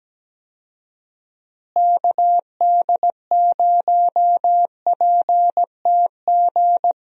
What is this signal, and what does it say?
Morse code, heard as a keyed tone, about 17 wpm, 700 Hz. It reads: KD0PTG
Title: USA Call Signs - 6 Characters - Repeated 3x - 17wpm